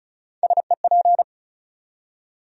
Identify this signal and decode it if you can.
Morse code, heard as a keyed tone, about 34 wpm, 700 Hz. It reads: SEP
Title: USA Call Sign Postfixes - 3 Letters 35wpm